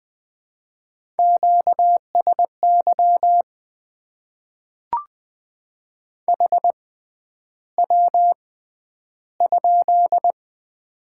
Morse code, heard as a keyed tone, about 20 wpm, spaced as for 10 wpm, 700 Hz. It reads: QSY HW?
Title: Top 90 QSO Elements - 6x Character Spacing - 20wpm